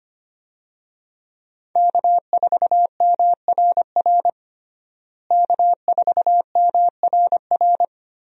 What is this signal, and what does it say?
Morse code, heard as a keyed tone, about 25 wpm, 700 Hz. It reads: K4MRR K4MRR
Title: USA Call Signs - 5 Characters - Repeated 3x - 25wpm